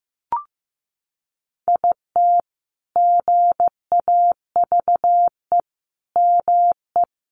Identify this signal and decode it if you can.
Morse code, heard as a keyed tone, about 15 wpm, 700 Hz. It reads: IT GAVE ME
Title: Sentences from Top 1300 Words 15wpm